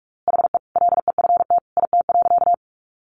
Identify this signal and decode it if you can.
Morse code, heard as a keyed tone, about 44 wpm, 700 Hz. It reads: HE LEFT IT.
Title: A Sound Of Thunder 45wpm